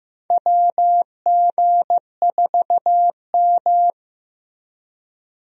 Morse code, heard as a keyed tone, about 16 wpm, 700 Hz. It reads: WG4M
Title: USA Call Signs - 4 Characters - Repeated 3x - 15wpm